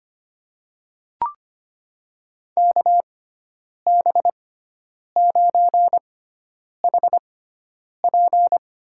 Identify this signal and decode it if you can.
Morse code, heard as a keyed tone, about 25 wpm, spaced as for 9 wpm, 700 Hz. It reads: KB9HP
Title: USA Call Signs - 6 Characters - 6x Character Spacing - 25wpm